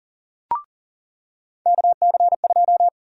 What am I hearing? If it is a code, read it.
Morse code, heard as a keyed tone, about 40 wpm, 700 Hz. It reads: KC2
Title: USA Call Sign Prefixes - 40wpm